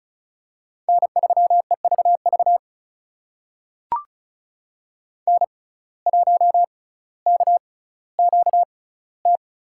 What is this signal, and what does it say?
Morse code, heard as a keyed tone, about 36 wpm, spaced as for 18 wpm, 700 Hz. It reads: N3EVV N1KQT
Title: USA Call Signs - 5 Characters - 6x Character Spacing - 35wpm